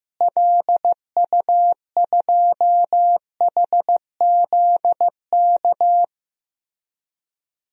Morse code, heard as a keyed tone, about 15 wpm, 700 Hz. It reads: LU2HZK
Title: DXCC-100 Call Signs - 6 Characters - Repeated 3x - 15wpm